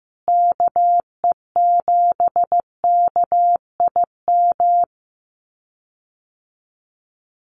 Morse code, heard as a keyed tone, about 15 wpm, 700 Hz. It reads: KE7KIM
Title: USA Call Signs Repeated 3x 15wpm